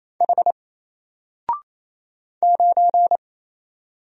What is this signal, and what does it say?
Morse code, heard as a keyed tone, about 27 wpm, 700 Hz. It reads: H 9
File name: Single-Letter-Number-28wpm